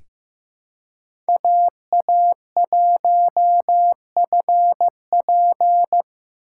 Morse code, heard as a keyed tone, about 15 wpm, 700 Hz. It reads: AA1FP
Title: USA Call Signs - 5 Characters - 4x Character Spacing - 15wpm